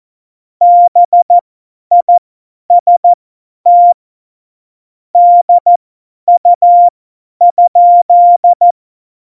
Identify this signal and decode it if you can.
Morse code, heard as a keyed tone, about 14 wpm, spaced as for 10 wpm, 700 Hz. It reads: BIST DU?
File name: Morsekurs / Witze / Witz 15. Dezember 2020, 50 F70